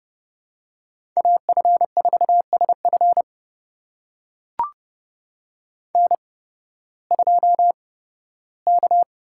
Morse code, heard as a keyed tone, about 30 wpm, spaced as for 16 wpm, 700 Hz. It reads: AF4SF N2K
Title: USA Call Signs - 5 Characters - 8x Character Spacing - 30wpm